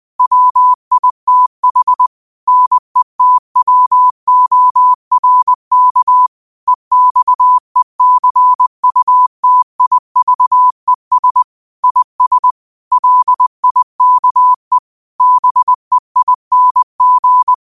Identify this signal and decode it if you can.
Morse code, heard as a keyed tone, about 20 wpm, 1000 Hz. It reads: WITH NETWORK EXECUTIVES IS LIKE BEING
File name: qotdfiles / 20wpm